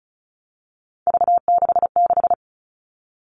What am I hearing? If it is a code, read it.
Morse code, heard as a keyed tone, about 34 wpm, 700 Hz. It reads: V66
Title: All Cut Numbers - 3-Digit 35wpm